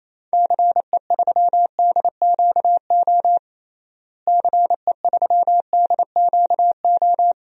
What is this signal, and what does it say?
Morse code, heard as a keyed tone, about 28 wpm, 700 Hz. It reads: CE3DQO CE3DQO
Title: DXCC-100 Call Signs - 6 Characters - Repeated 2x - 28wpm